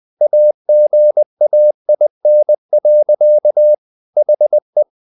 Morse code, heard as a keyed tone, about 20 wpm, 600 Hz. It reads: AGAIN. HE
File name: the old man and the sea part 2 20wpm